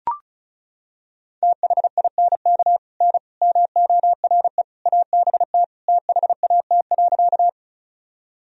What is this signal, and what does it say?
Morse code, heard as a keyed tone, about 34 wpm, 700 Hz. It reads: THINK N MORE ABT THAT.
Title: Sentences from Top 200 Words and Abbreviations 35wpm